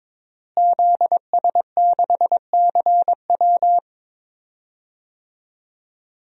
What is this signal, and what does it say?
Morse code, heard as a keyed tone, about 21 wpm, 700 Hz. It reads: ZS6CW
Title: DXCC-100 Call Signs - 5 Characters - Repeated 2x - 22wpm